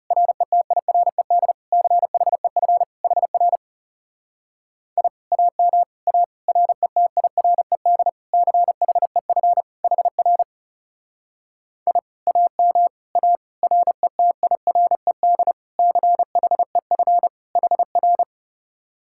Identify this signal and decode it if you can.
Morse code, heard as a keyed tone, about 39 wpm, 700 Hz. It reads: RETIRED CHEF HR I AM A RETIRED CHEF HR I AM A RETIRED CHEF HR